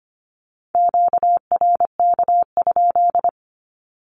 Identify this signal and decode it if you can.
Morse code, heard as a keyed tone, about 25 wpm, 700 Hz. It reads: QRK?
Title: Q-Codes 25wpm